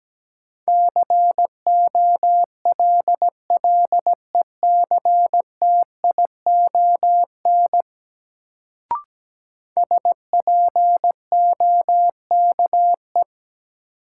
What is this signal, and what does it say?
Morse code, heard as a keyed tone, about 17 wpm, 700 Hz. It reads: COLLECTION SPOKE
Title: Sentences from Top 1200 Words - New Word Drill Alt - 17wpm